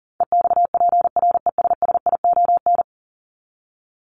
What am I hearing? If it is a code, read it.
Morse code, heard as a keyed tone, about 37 wpm, 700 Hz. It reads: EXPRESSION
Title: Top 1001-1250 Words 40wpm